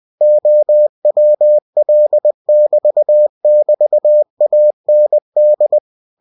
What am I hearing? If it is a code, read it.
Morse code, heard as a keyed tone, about 20 wpm, 600 Hz. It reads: OWL<BT><BT>AND